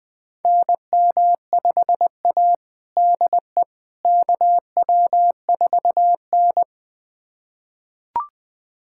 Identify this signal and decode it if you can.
Morse code, heard as a keyed tone, about 19 wpm, 700 Hz. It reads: NM5A DE KW4N
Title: USA Callsign DE Callsign - 4 Characters - 20wpm